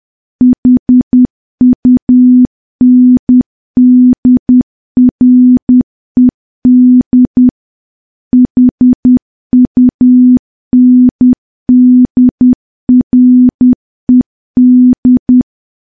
Morse code, heard as a keyed tone, about 10 wpm, 250 Hz. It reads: HUNDRED HUNDRED
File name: morse / morse-training-10WPM